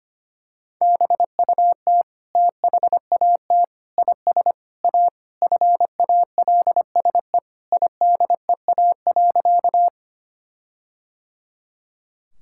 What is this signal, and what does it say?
Morse code, heard as a keyed tone, about 25 wpm, 700 Hz. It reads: BUT THAT IS A FALSE IDEA.
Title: Sentences from Top 1600 Words 25wpm